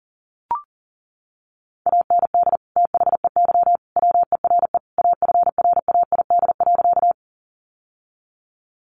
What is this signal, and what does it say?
Morse code, heard as a keyed tone, about 40 wpm, 700 Hz. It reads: AND THEY WERE AFRAID.
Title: Sentences from Top 1500 Words 40wpm